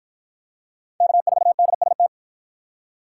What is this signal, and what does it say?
Morse code, heard as a keyed tone, about 53 wpm, 700 Hz. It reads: K4DST